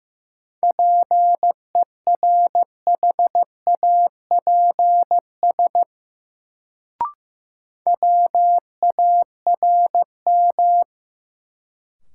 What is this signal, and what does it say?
Morse code, heard as a keyed tone, about 15 wpm, 700 Hz. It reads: PERHAPS WARM
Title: Sentences from Top 500 Words - New Word Drill - 15wpm